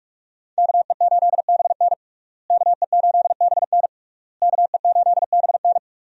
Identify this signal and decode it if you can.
Morse code, heard as a keyed tone, about 42 wpm, 700 Hz. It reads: KE8BN KE8BN KE8BN
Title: USA Call Signs - 5 Characters - Repeated 3x - 45wpm